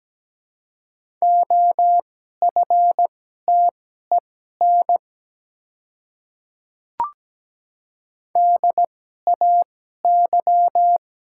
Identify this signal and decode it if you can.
Morse code, heard as a keyed tone, about 17 wpm, spaced as for 12 wpm, 700 Hz. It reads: OFTEN DAY
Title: 2x Character Spacing - Top 300 Words - 17wpm